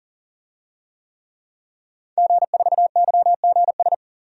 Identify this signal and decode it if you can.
Morse code, heard as a keyed tone, about 40 wpm, 700 Hz. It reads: G4YGS
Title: DXCC-100 Call Signs - 5 Characters - 40wpm